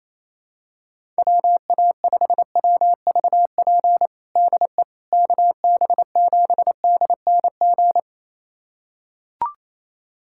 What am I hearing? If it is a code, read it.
Morse code, heard as a keyed tone, about 28 wpm, 700 Hz. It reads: WA5WVP DE KB7DNG
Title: USA Callsign DE Callsign - 6 Characters - 28wpm